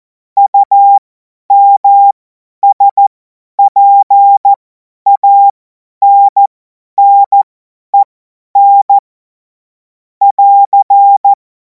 Morse code, heard as a keyed tone, about 14 wpm, spaced as for 10 wpm, 800 Hz. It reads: UMSPANNEN <AR>
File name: Morsekurs / Witze / Witz 31. März 2026 50 F70